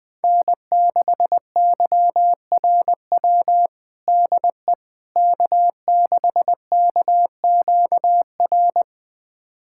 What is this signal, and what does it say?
Morse code, heard as a keyed tone, about 20 wpm, 700 Hz. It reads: N6YRW DE K6KQR
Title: USA Callsign DE Callsign - 5 Characters - 20wpm